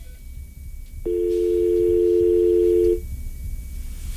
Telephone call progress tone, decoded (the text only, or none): dial tone